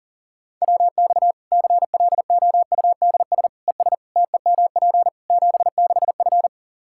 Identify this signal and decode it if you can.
Morse code, heard as a keyed tone, about 40 wpm, 700 Hz. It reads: WX CLOUDS ES TEMP 76F
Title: K1USN SST QSO Practice 40wpm